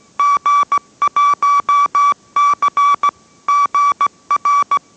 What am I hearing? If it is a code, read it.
Morse code, heard as a keyed tone, about 17 wpm, 1150 Hz. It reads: G1C GR